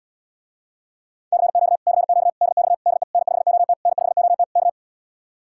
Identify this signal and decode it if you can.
Morse code, heard as a keyed tone, about 80 wpm, 700 Hz. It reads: CQ CQ DX DE N5OT N5OT K